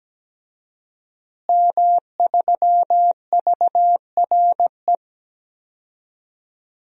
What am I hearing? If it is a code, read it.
Morse code, heard as a keyed tone, about 17 wpm, 700 Hz. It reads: M3VRE
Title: DXCC-100 Call Signs - 5 Characters - Repeated 3x - 17wpm